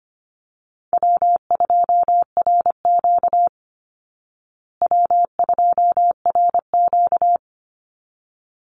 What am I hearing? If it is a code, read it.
Morse code, heard as a keyed tone, about 25 wpm, 700 Hz. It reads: W2RQ W2RQ
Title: Easy Call Signs Repeated 2x - 25wpm